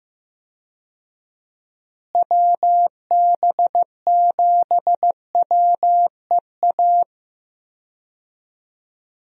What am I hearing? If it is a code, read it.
Morse code, heard as a keyed tone, about 15 wpm, 700 Hz. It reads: WB7WEA
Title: USA Call Signs - 6 Characters - Repeated 2x - 15wpm